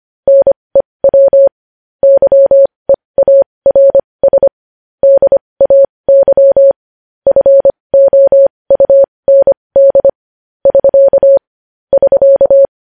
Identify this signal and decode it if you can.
Morse code, heard as a keyed tone, about 25 wpm, spaced as for 20 wpm, 550 Hz. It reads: NEW YEARS DAY FOUND <SK> <SK>
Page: news-headlines-in-morse-code-at-20-wpm